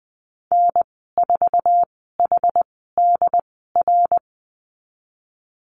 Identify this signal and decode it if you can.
Morse code, heard as a keyed tone, about 20 wpm, spaced as for 14 wpm, 700 Hz. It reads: N4HDR
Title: 069 Lesson 12g - Letter d - Call Signs 20wpm